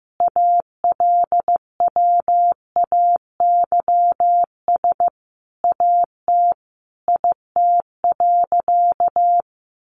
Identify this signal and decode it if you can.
Morse code, heard as a keyed tone, about 15 wpm, 700 Hz. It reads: ALWAYS AT IT.
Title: Sentences from Top 300 Words 15wpm